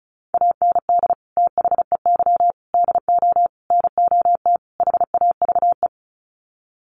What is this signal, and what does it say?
Morse code, heard as a keyed tone, about 34 wpm, 700 Hz. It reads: AND THEY DO NOT HAVE
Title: Sets of 5 Words Encore2 35wpm